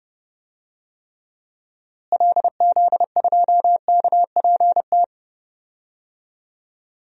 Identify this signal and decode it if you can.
Morse code, heard as a keyed tone, about 29 wpm, 700 Hz. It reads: LZ2KPT
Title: DXCC-100 Call Signs - 6 Characters - Repeated 3x - 30wpm